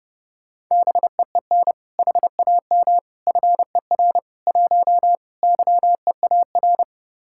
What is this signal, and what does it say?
Morse code, heard as a keyed tone, about 29 wpm, 700 Hz. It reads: BEEN HAM FER 1 YEAR